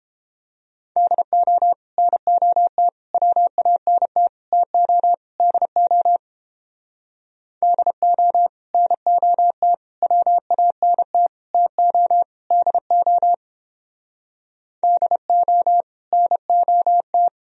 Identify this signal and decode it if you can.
Morse code, heard as a keyed tone, about 29 wpm, 700 Hz. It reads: DO NOT WANT TO DO DO NOT WANT TO DO DO NOT